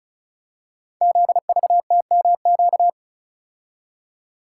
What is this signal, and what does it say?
Morse code, heard as a keyed tone, about 36 wpm, 690 Hz. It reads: ZVTMQ
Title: Code Groups - 5 Letters - 35wpm